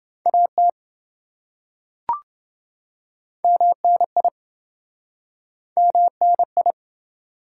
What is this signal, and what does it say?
Morse code, heard as a keyed tone, about 28 wpm, 700 Hz. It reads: AT MNI MNI